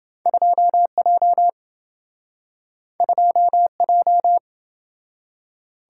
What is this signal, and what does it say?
Morse code, heard as a keyed tone, about 27 wpm, 700 Hz. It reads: 2J 2J